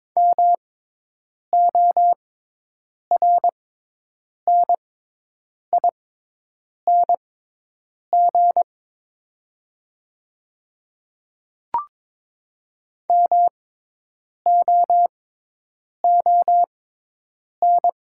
Morse code, heard as a keyed tone, about 22 wpm, spaced as for 8 wpm, 700 Hz. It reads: MORNING MOON
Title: 6x Character Spacing - Top 500 Words - 22wpm